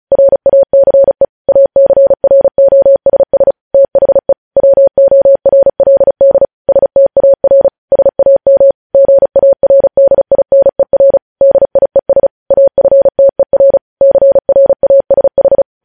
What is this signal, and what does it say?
Morse code, heard as a keyed tone, about 36 wpm, 550 Hz. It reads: RACE ACROSS THE WORLD STAR SAM GARDINER DIES AFTER CRASH
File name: podcast / News headlines-1x-35x35